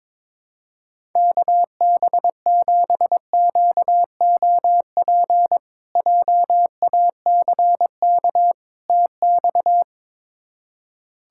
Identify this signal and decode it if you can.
Morse code, heard as a keyed tone, about 22 wpm, 700 Hz. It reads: KB7QOP JACK TX
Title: K1USN SST QSO Practice - 22wpm